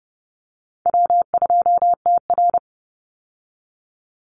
Morse code, heard as a keyed tone, about 30 wpm, 700 Hz. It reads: W2TR
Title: Easy Call Signs Repeated 2x - 30wpm